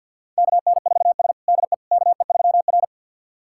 Morse code, heard as a keyed tone, about 51 wpm, 700 Hz. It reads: KN4S DE KE3R
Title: USA Callsign DE Callsign - 4 Characters - 50wpm